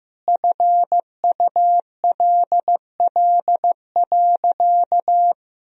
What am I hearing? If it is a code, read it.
Morse code, heard as a keyed tone, about 15 wpm, 700 Hz. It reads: FULL.
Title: Sentences from Top 1000 Words - Review Part 3 - 15wpm